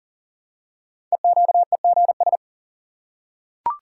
Morse code, heard as a keyed tone, about 40 wpm, 700 Hz. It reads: EQEGS
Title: Code Groups - 5 Letters - 40wpm